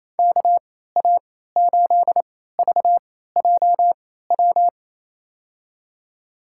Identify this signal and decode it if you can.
Morse code, heard as a keyed tone, about 29 wpm, spaced as for 16 wpm, 690 Hz. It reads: KA8VJW
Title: USA Call Signs - 3x Character Spacing - 28wpm